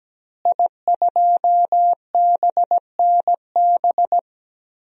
Morse code, heard as a keyed tone, about 17 wpm, 700 Hz. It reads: I2BNB
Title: DXCC-100 Call Signs - 5 Characters - Repeated 2x - 17wpm